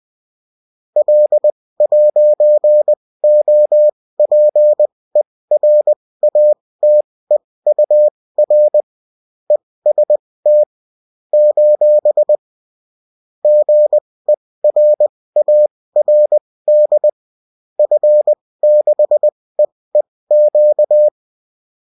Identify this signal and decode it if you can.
Morse code, heard as a keyed tone, about 20 wpm, spaced as for 16 wpm, 600 Hz. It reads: L'OPERATEUR EST : GERARD F6EEQ